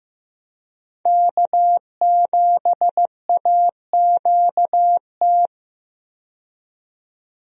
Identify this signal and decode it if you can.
Morse code, heard as a keyed tone, about 15 wpm, 700 Hz. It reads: K7AQT
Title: USA Call Signs - 5 Characters - 15wpm